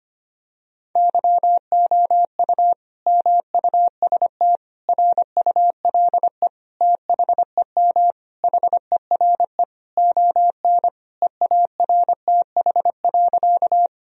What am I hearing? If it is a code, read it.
Morse code, heard as a keyed tone, about 25 wpm, 700 Hz. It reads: YOU MUST RULE THEM HERE ON EARTH.
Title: Sentences from Top 1000 Words - Review Part 2 - 25wpm